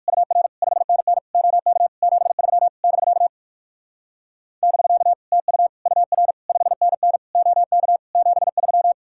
Morse code, heard as a keyed tone, about 55 wpm, 700 Hz. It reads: UR 5NN OK 73 <BK> <BK> TU UR 5NN OK 73
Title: Speed-Racing Common QSO Phrases Encore - 45wpm